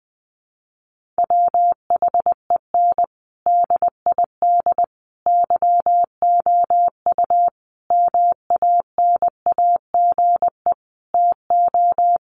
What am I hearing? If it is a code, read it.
Morse code, heard as a keyed tone, about 20 wpm, 700 Hz. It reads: WHEN DID YOU MANAGE TO
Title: Sentences from Top 1400 Words 20wpm